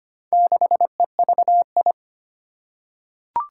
Morse code, heard as a keyed tone, about 25 wpm, 700 Hz. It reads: 6EVI
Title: Four Challenging Letter-Number Combinations 25wpm